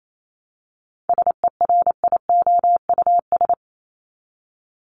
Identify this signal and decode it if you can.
Morse code, heard as a keyed tone, about 28 wpm, 700 Hz. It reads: SERIOUS
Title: LY Suffix - Top 100 Words with Base Words 28wpm